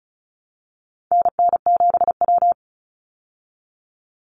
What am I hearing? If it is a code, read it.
Morse code, heard as a keyed tone, about 34 wpm, 700 Hz. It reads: NN7W
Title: Call Signs Repeated 3x - 35wpm